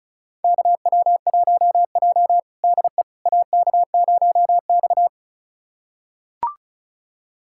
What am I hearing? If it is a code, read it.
Morse code, heard as a keyed tone, about 36 wpm, 700 Hz. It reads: KW1J DE AK0X
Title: USA Callsign DE Callsign - 4 Characters - 35wpm